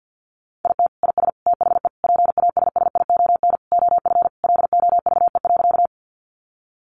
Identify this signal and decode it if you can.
Morse code, heard as a keyed tone, about 50 wpm, 700 Hz. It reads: IT IS THE PASSION OF LOVE.
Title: Sentences from Top 1300 Words 50wpm